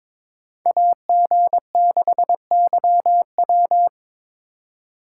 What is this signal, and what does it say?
Morse code, heard as a keyed tone, about 26 wpm, 700 Hz. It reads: AG6YW
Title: Speed Racing - USA 2x2 Call Signs 22wpm